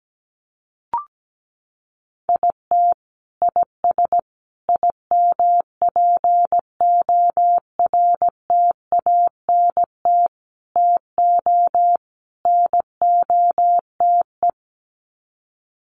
Morse code, heard as a keyed tone, about 17 wpm, 700 Hz. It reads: IT IS IMPORTANT TO NOTE
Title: Sets of 5 Words 17wpm